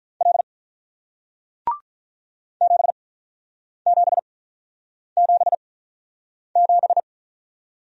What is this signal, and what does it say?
Morse code, heard as a keyed tone, about 48 wpm, spaced as for 9 wpm, 700 Hz. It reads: L 7777